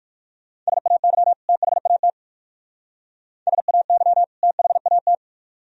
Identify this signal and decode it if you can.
Morse code, heard as a keyed tone, about 49 wpm, 700 Hz. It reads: SAY THAT SAY THAT
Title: Speed-Racing - Sets of 2 Words 45wpm